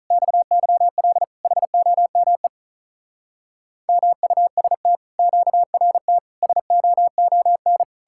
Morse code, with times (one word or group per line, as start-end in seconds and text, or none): 0.1-1.2 XYL
1.4-2.5 HOME
3.9-5.0 MUST
5.2-6.2 QRT
6.4-7.8 SOON